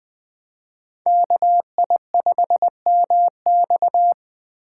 Morse code, heard as a keyed tone, about 20 wpm, 700 Hz. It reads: KI5MX
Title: Speed Racing - USA 2x2 Call Signs 17wpm